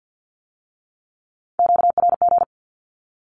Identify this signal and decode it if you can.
Morse code, heard as a keyed tone, about 50 wpm, 700 Hz. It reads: QRG
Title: Q-Codes with Spoken Meaning - 50wpm